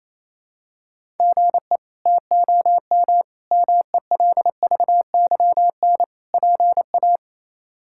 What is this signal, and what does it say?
Morse code, heard as a keyed tone, about 28 wpm, 700 Hz. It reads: GE TOM MELVYN PA
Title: K1USN SST QSO Practice - 28wpm